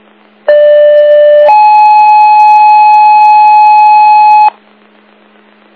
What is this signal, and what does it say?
Two-tone page, heard as a one-second tone, then a three-second tone, about 590 then 850 Hz. It is heard from 0.5 to 4.5 s.